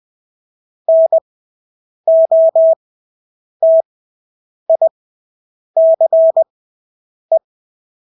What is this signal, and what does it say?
Morse code, heard as a keyed tone, about 20 wpm, spaced as for 8 wpm, 650 Hz. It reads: NOTICE
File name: Engelske ord 20/8 wpm